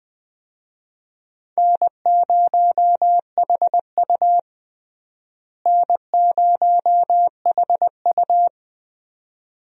Morse code, heard as a keyed tone, about 20 wpm, 700 Hz. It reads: N0HU N0HU
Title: USA Call Signs - 4 Characters - Repeated 2x - 20wpm